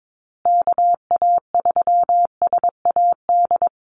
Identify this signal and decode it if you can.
Morse code, heard as a keyed tone, about 22 wpm, 700 Hz. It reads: KA3SAD
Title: USA Call Signs Repeated 2x 22wpm